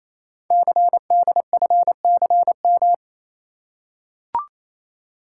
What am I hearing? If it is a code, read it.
Morse code, heard as a keyed tone, about 29 wpm, 700 Hz. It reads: CDFCM
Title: Code Groups - 5 Letters - 28wpm